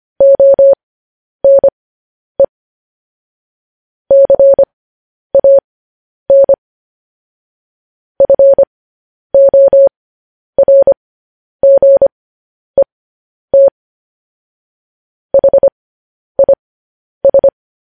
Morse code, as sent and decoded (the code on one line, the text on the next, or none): --- -. / -.-. .- -. / ..-. --- .-. --. - / .... .. ...
ONE CAN FORGET HIS